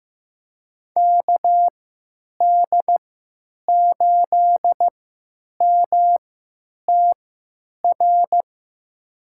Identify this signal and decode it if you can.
Morse code, heard as a keyed tone, about 16 wpm, spaced as for 9 wpm, 700 Hz. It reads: KD8MTR
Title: USA Call Signs - 3x Character Spacing - 15wpm